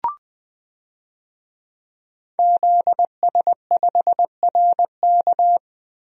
Morse code, heard as a keyed tone, about 20 wpm, 700 Hz. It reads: ZS5RK
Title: DXCC-100 Call Signs - 5 Characters - Repeated 2x - 20wpm